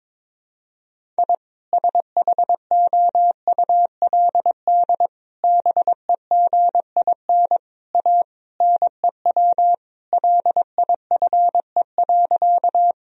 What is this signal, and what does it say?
Morse code, heard as a keyed tone, about 22 wpm, 700 Hz. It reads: I SHOULD BEGIN A NEW LIFE.
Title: Sentences from Top 1000 Words - Review Part 1 - 22wpm